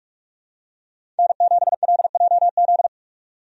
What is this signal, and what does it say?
Morse code, heard as a keyed tone, about 46 wpm, 700 Hz. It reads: N7LJZ
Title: USA Call Signs - 5 Characters - 45wpm